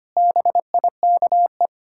Morse code, heard as a keyed tone, about 25 wpm, 700 Hz. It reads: BIKE
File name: practice / 25wpm / Difficult Sentences with Hints - 25wpm